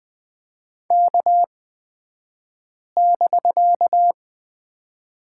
Morse code, heard as a keyed tone, about 20 wpm, spaced as for 5 wpm, 700 Hz. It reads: K<BK>
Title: 249 Lesson 40d - BK - Two Characters 20wpm